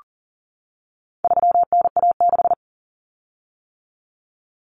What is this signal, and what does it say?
Morse code, heard as a keyed tone, about 40 wpm, 700 Hz. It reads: 3NA6
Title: Common Cut Numbers - 4-Digit 40wpm